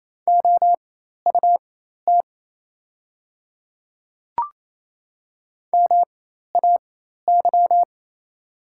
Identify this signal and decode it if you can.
Morse code, heard as a keyed tone, about 28 wpm, spaced as for 13 wpm, 700 Hz. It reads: OUT MAY
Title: 4x Character Spacing - Top 100 Words - 28wpm